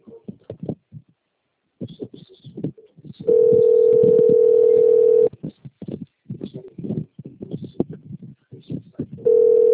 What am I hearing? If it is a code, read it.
Ringback tone (North America), 440 + 480 Hz.